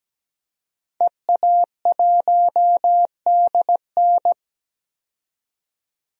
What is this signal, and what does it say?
Morse code, heard as a keyed tone, about 17 wpm, 700 Hz. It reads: EA1DN
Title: DXCC-100 Call Signs - 5 Characters - 17wpm